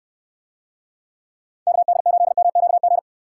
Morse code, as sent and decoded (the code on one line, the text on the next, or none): -.- -.. ---.. -- --.- --.
KD8MQG